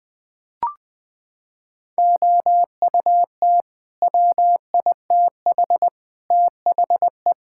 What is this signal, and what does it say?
Morse code, heard as a keyed tone, about 20 wpm, 700 Hz. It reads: OUT WITH THE
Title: Sets of 9 Words - 20wpm